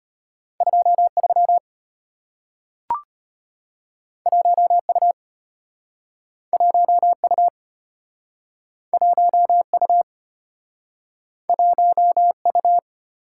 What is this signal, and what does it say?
Morse code, heard as a keyed tone, about 34 wpm, 700 Hz. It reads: 23 1U 1U 1U 1U